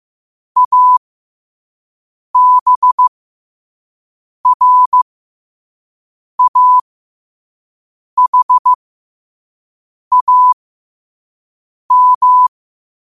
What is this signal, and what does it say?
Morse code, heard as a keyed tone, about 15 wpm, spaced as for 5 wpm, 1000 Hz. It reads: ABRAHAM
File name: 5wpm